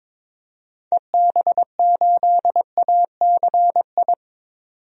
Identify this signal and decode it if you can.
Morse code, heard as a keyed tone, about 21 wpm, 700 Hz. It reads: EB8ACI
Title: DXCC-100 Call Signs - 6 Characters - 22wpm